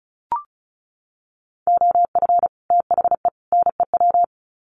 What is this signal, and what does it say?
Morse code, heard as a keyed tone, about 34 wpm, 700 Hz. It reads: OF THE NEW
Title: Sets of 3 Words 35wpm